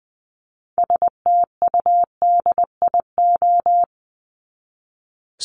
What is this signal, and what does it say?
Morse code, heard as a keyed tone, about 20 wpm, 700 Hz. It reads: STUDIO
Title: Top 1001-1250 Words 20wpm